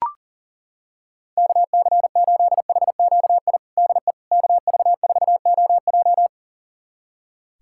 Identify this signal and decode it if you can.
Morse code, heard as a keyed tone, about 40 wpm, 700 Hz. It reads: KC8HQI DE KV4OJ
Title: USA Callsign DE Callsign 40wpm